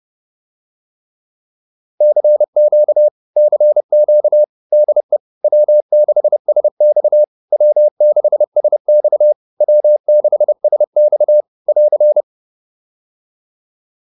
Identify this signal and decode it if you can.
Morse code, heard as a keyed tone, about 30 wpm, 600 Hz. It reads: CQ CQ DE W6SX W6SX W6SX <AR>